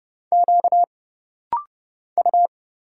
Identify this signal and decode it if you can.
Morse code, heard as a keyed tone, about 30 wpm, 700 Hz. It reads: Q U
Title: Single Letter-Number Alt - Rapid-Fire 30wpm